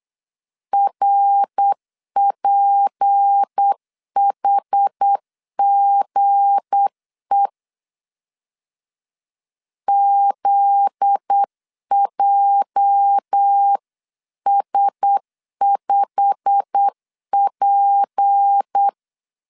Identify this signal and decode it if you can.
Morse code, heard as a keyed tone, about 8 wpm, 800 Hz. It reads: RPHGE ZJS5P
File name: CW / DET / 1 9 1500C S B 8wpm 5624